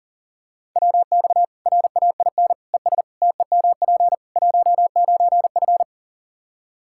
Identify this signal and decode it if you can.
Morse code, heard as a keyed tone, about 40 wpm, 700 Hz. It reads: WX RAIN ES TEMP 19F
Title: Practice QSO - Exchange 2 - Info - 40wpm